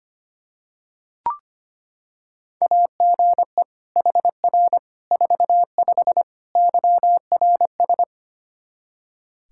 Morse code, heard as a keyed tone, about 25 wpm, 700 Hz. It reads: AGE HR 45 YRS
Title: Practice QSO - Exchange 3 - More Info - 25wpm